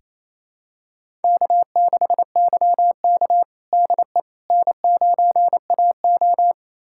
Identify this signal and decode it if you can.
Morse code, heard as a keyed tone, about 28 wpm, 700 Hz. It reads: K6YK DE N9AO
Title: USA Callsign DE Callsign - 4 Characters - 28wpm